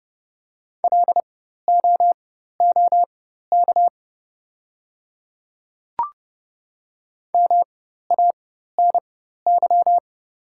Morse code, heard as a keyed tone, about 29 wpm, spaced as for 14 wpm, 700 Hz. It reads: LOOK MANY
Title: 4x Character Spacing - Top 100 Words - 30wpm